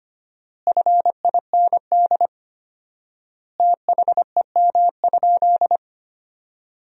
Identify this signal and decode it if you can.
Morse code, heard as a keyed tone, about 25 wpm, 700 Hz. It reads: FIND THEM?
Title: 4x Word-Spacing - Sentences from Top 200 Words - 25wpm